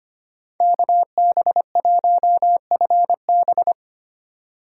Morse code, heard as a keyed tone, about 25 wpm, 700 Hz. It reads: KB1FB
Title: USA Call Signs - 5 Characters - 3x Character Spacing - 25wpm